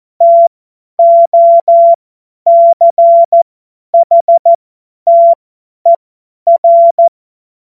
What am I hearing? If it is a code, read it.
Morse code, heard as a keyed tone, about 14 wpm, spaced as for 10 wpm, 680 Hz. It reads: TOCHTER
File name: Morsekurs / Witze / Witz 10. Januar 2023 50 F70